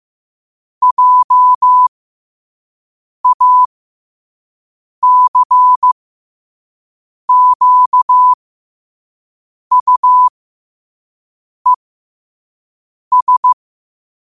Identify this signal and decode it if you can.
Morse code, heard as a keyed tone, about 15 wpm, spaced as for 5 wpm, 1000 Hz. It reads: JACQUES
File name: qotdfiles / 5wpm